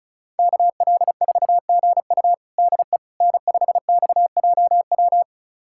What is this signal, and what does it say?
Morse code, heard as a keyed tone, about 36 wpm, 700 Hz. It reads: KL4GU DE N5XJW
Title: USA Callsign DE Callsign - 5 Characters - 35wpm